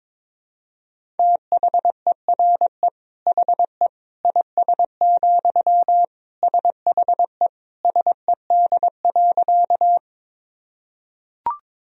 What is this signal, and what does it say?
Morse code, heard as a keyed tone, about 21 wpm, 700 Hz. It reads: THERE HE IS, SHE SED.
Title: Sentences from Top 100 Words and Abbreviations 22wpm